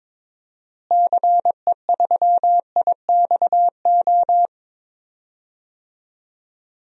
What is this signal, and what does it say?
Morse code, heard as a keyed tone, about 22 wpm, 700 Hz. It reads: CE3IXO
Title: DXCC-100 Call Signs - 6 Characters - Repeated 2x - 22wpm